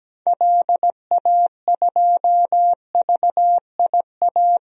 Morse code, heard as a keyed tone, about 17 wpm, 700 Hz. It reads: LA2VIA
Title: DXCC-100 Call Signs - 6 Characters - Repeated 3x - 17wpm